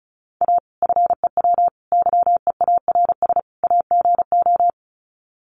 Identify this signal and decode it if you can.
Morse code, heard as a keyed tone, about 34 wpm, 700 Hz. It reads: A FEW YEARS AGO
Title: Sets of 4 Words 35wpm